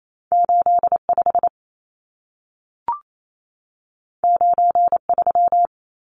Morse code, heard as a keyed tone, about 28 wpm, 700 Hz. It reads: 85 93
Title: Numbers - 2-Digit 28wpm